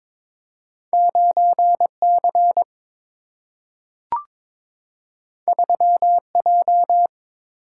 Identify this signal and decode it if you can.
Morse code, heard as a keyed tone, about 22 wpm, 700 Hz. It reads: 9 C 3 J
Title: Single-Letter Single-Number 22wpm